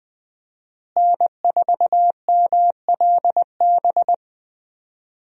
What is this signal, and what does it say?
Morse code, heard as a keyed tone, about 20 wpm, 700 Hz. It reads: N4MLB
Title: USA Call Signs - 5 Characters - Repeated 3x - 20wpm